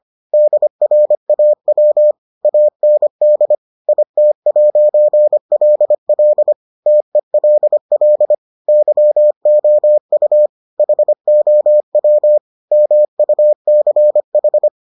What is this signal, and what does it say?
Morse code, heard as a keyed tone, about 25 wpm, 600 Hz. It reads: DRAW AND IT'LL TELL YOU HOW MUCH